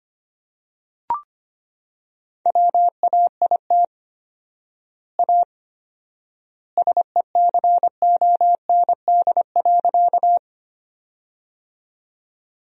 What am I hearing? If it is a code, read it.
Morse code, heard as a keyed tone, about 25 wpm, 700 Hz. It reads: WAIT A SECOND.
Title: 4x Word-Spacing - Sentences from Top 500 Words - 25wpm